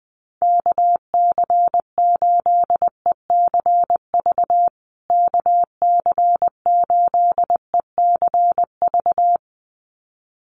Morse code, heard as a keyed tone, about 19 wpm, 700 Hz. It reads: KC8ECV KC8ECV
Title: USA Call Signs Repeated 2x 20wpm